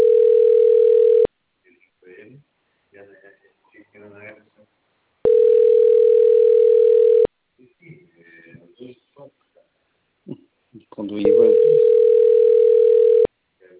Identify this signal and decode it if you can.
Ringback tone (North America), 440 + 480 Hz, with speech.
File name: out- - - - - .57529